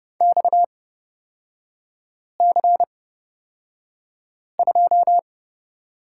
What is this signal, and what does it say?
Morse code, heard as a keyed tone, about 30 wpm, spaced as for 5 wpm, 700 Hz. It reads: XC2
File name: Lesson 37e - Letter x - Three Characters 30wpm